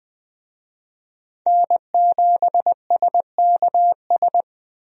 Morse code, heard as a keyed tone, about 20 wpm, 700 Hz. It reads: N7SKS
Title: USA Call Signs - 5 Characters - Repeated 2x - 20wpm